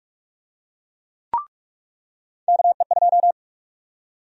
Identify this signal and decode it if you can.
Morse code, heard as a keyed tone, about 44 wpm, 700 Hz. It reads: KE2
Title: USA Call Sign Prefixes - 45wpm